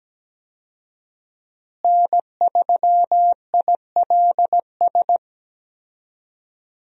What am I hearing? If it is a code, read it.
Morse code, heard as a keyed tone, about 17 wpm, 700 Hz. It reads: N3ILS
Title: USA Call Signs - 5 Characters - Repeated 2x - 17wpm